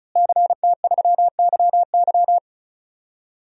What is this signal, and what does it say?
Morse code, heard as a keyed tone, about 36 wpm, 700 Hz. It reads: CT3YY
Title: DXCC-100 Call Signs - 5 Characters - Repeated 3x - 35wpm